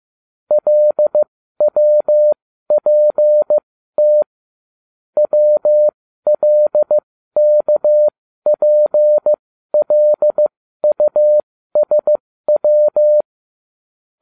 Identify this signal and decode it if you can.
Morse code, heard as a keyed tone, about 15 wpm, spaced as for 12 wpm, 600 Hz. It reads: LWPT WLKPLUSW